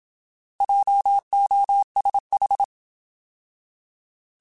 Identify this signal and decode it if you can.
Morse code, heard as a keyed tone, about 27 wpm, 800 Hz. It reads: JOSH
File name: Alphabetical / Alphabetical mp3s / Js @25